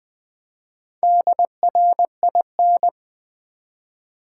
Morse code, heard as a keyed tone, about 20 wpm, 700 Hz. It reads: DRIN
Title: Build A Word - 1 Letter At A Time - 3 to 5-Letter Words - 20wpm